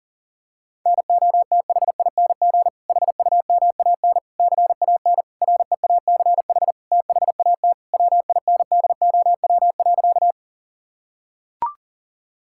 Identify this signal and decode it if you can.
Morse code, heard as a keyed tone, about 40 wpm, 700 Hz. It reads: NOTHING HUMAN CAN REACH THAT WINDOW.
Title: Sentences from Top 1600 Words 40wpm